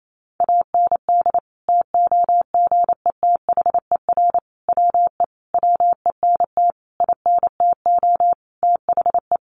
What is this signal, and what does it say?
Morse code, heard as a keyed tone, about 28 wpm, 700 Hz. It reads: AND TOGETHER WE WENT INTO THE
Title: Sentences from Top 600 Words 28wpm